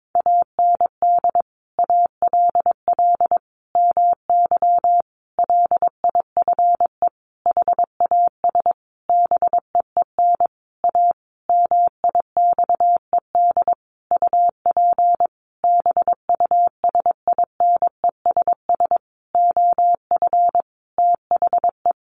Morse code, heard as a keyed tone, about 22 wpm, 700 Hz. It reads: AND ALL MY LIFE HAS BEEN A MIXED UP BUSINESS OF THE
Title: Sentences from Top 1500 Words 22wpm